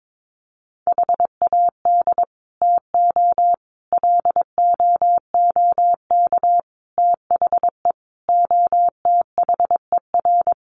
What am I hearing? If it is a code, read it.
Morse code, heard as a keyed tone, about 22 wpm, 700 Hz. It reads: HAD TO LOOK THE OTHER
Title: Sentences from Top 100 Words 22wpm